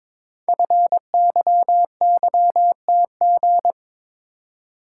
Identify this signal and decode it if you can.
Morse code, heard as a keyed tone, about 22 wpm, 700 Hz. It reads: FYYTG
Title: Code Groups - 5 Letters - 22wpm